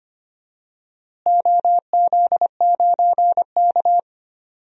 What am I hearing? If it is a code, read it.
Morse code, heard as a keyed tone, about 25 wpm, 700 Hz. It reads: OZ9K
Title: DXCC-100 Call Signs - 4 Characters - Repeated 3x - 25wpm